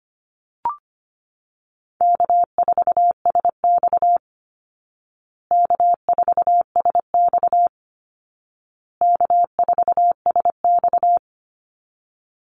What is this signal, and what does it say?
Morse code, heard as a keyed tone, about 25 wpm, 700 Hz. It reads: K4SX K4SX K4SX
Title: Call Signs Repeated 2x - 25wpm